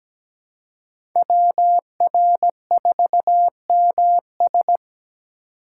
Morse code, heard as a keyed tone, about 17 wpm, 700 Hz. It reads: WR4MS
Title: USA Call Signs - 5 Characters - 2x Character Spacing - 17wpm